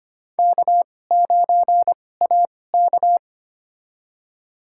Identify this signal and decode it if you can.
Morse code, heard as a keyed tone, about 25 wpm, spaced as for 18 wpm, 700 Hz. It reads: K9AK
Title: USA Call Signs - 4 Characters - 2x Character Spacing - 25wpm